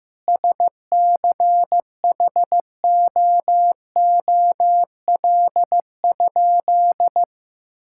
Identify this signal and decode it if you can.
Morse code, heard as a keyed tone, about 15 wpm, 700 Hz. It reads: SCHOOL?
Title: Sentences from Top 1000 Words - Review Part 6 - 15wpm